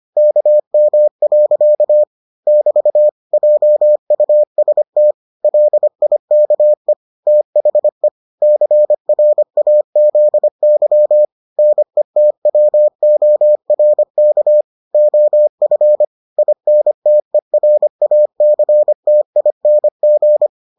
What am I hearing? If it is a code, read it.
Morse code, heard as a keyed tone, about 25 wpm, 600 Hz. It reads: KM. <BT> JUST LIKE THE CRAZY NETWORK OF INTERACTING